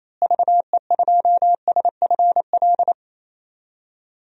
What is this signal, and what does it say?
Morse code, heard as a keyed tone, about 28 wpm, 700 Hz. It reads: VE2SFL
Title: DXCC-100 Call Signs - 6 Characters - Repeated 3x - 28wpm